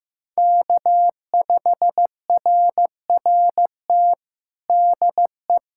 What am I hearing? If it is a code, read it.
Morse code, heard as a keyed tone, about 15 wpm, 700 Hz. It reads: K5RRT DE
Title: USA Callsign DE Callsign 15wpm